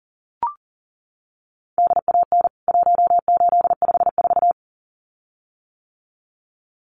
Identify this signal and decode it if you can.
Morse code, heard as a keyed tone, about 40 wpm, 700 Hz. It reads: DAN 1854
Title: CWT Exchange 40wpm